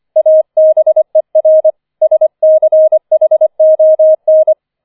Morse code, heard as a keyed tone, about 25 wpm, 630 Hz. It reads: ABER SCHON
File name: 004 Vierter Streich 25wpm